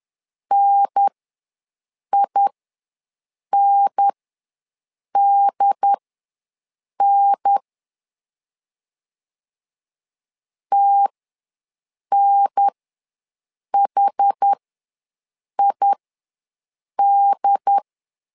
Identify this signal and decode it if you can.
Morse code, heard as a keyed tone, about 10 wpm, spaced as for 6 wpm, 800 Hz. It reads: NINDN TNHID
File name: LEZIONI / 2 UHDN / 2 UHDN ws 10wpm 2044